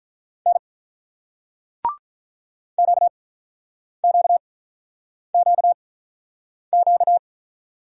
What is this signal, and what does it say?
Morse code, heard as a keyed tone, about 48 wpm, spaced as for 9 wpm, 700 Hz. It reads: N QQQQ